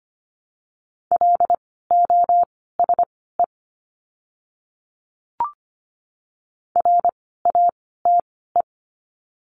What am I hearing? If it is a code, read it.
Morse code, heard as a keyed tone, about 25 wpm, spaced as for 16 wpm, 700 Hz. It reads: LOSE RATE
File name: course / 25:10wpm / 094 Lesson 16f - Letter c - Words 25wpm